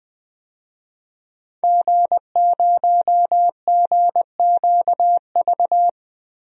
Morse code, heard as a keyed tone, about 20 wpm, 700 Hz. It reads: G0GQV
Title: DXCC-100 Call Signs - 5 Characters - 20wpm